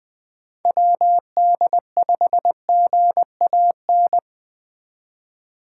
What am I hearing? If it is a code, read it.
Morse code, heard as a keyed tone, about 20 wpm, 700 Hz. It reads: WD5GAN